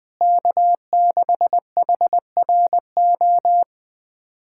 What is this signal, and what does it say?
Morse code, heard as a keyed tone, about 21 wpm, 700 Hz. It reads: K6HRO
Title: USA Callsign DE Callsign 20wpm